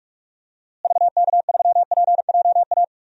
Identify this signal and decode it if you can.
Morse code, heard as a keyed tone, about 43 wpm, 700 Hz. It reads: VK3PJA
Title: DXCC-100 Call Signs - 6 Characters - Repeated 2x - 45wpm